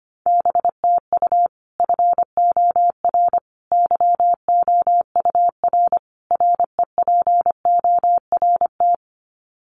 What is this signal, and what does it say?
Morse code, heard as a keyed tone, about 25 wpm, 700 Hz. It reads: BTU FOR YOUR REPORT
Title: Common QSO Phrases 25wpm